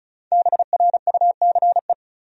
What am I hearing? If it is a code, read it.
Morse code, heard as a keyed tone, about 34 wpm, 700 Hz. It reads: BRUCE
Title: USA Common Names - Max 5 Letters - 35wpm